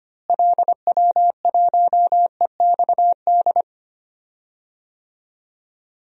Morse code, heard as a keyed tone, about 25 wpm, 700 Hz. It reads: LW1EXD
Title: DXCC-100 Call Signs - 6 Characters - Repeated 2x - 25wpm